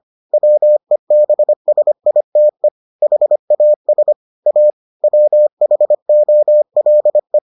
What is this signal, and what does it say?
Morse code, heard as a keyed tone, about 25 wpm, 600 Hz. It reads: WEBSITE HAS A WHOLE